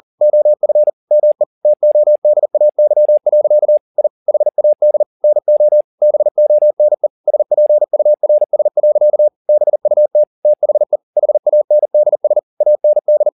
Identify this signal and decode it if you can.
Morse code, heard as a keyed tone, about 40 wpm, 600 Hz. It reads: OF ME TODAY. I HAD NO BONE SPURS. BUT THE HANDS AND
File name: the old man and the sea 2 40wpm